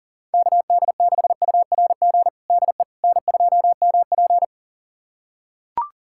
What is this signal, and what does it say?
Morse code, heard as a keyed tone, about 37 wpm, 700 Hz. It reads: KD6URG DE N2MP
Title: USA Callsign DE Callsign 40wpm